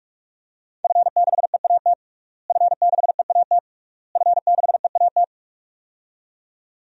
Morse code, heard as a keyed tone, about 46 wpm, 700 Hz. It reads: F6EAT F6EAT F6EAT
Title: DXCC-100 Call Signs - 5 Characters - Repeated 3x - 45wpm